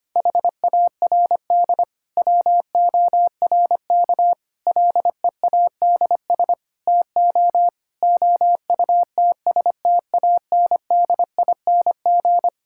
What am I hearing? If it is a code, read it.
Morse code, heard as a keyed tone, about 25 wpm, 700 Hz. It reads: HARD WORK LEADS TO OUTSTANDING
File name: practice / 25wpm / Difficult Sentences with Hints - 25wpm